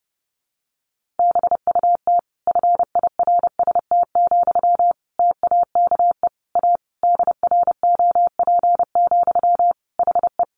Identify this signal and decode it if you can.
Morse code, heard as a keyed tone, about 30 wpm, 700 Hz. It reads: BUT FIRST, TAKE A DROP, HE
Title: Sentences from Top 1400 Words 30wpm